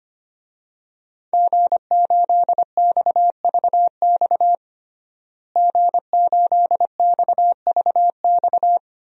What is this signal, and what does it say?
Morse code, heard as a keyed tone, about 25 wpm, 700 Hz. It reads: G8XVX G8XVX
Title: DXCC-100 Call Signs - 5 Characters - Repeated 2x - 25wpm